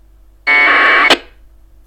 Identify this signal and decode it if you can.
Packet radio (AX.25, as in APRS).